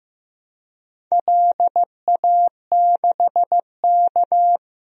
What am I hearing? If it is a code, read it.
Morse code, heard as a keyed tone, about 15 wpm, 700 Hz. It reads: LA6K
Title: DXCC-100 Call Signs - 4 Characters - 15wpm